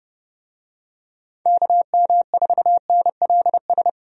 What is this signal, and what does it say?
Morse code, heard as a keyed tone, about 30 wpm, 700 Hz. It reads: KM4NLS
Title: USA Call Signs - 6 Characters - 30wpm